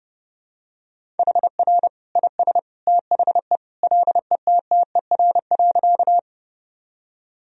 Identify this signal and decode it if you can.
Morse code, heard as a keyed tone, about 30 wpm, 700 Hz. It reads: HR IS THE LETTER.
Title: Sentences from Top 300 Words and Abbreviations 30wpm